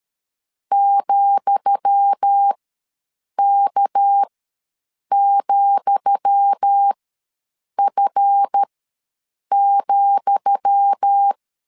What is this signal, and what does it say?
Morse code, heard as a keyed tone, about 13 wpm, spaced as for 7 wpm, 800 Hz. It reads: ,K,F,
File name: LEZIONI / 10 SIGNS / 10D COMA ws 12wpm 3618